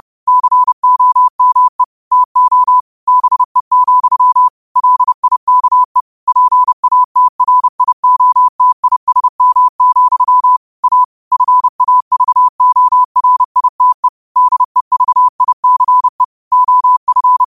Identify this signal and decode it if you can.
Morse code, heard as a keyed tone, about 30 wpm, 1000 Hz. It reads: COME TO BE, LIKE PATRIOTISM, A FAVORITE DEVICE OF
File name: qotdfiles / 30wpm